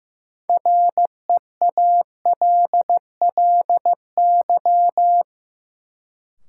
Morse code, heard as a keyed tone, about 15 wpm, 700 Hz. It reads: REALLY
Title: Sentences from Top 600 Words - New Word Drill - 15wpm